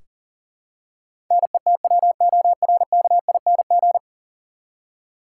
Morse code, heard as a keyed tone, about 40 wpm, 700 Hz. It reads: NETWORKING
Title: Practice QSO - Exchange 3 - More Info - 40wpm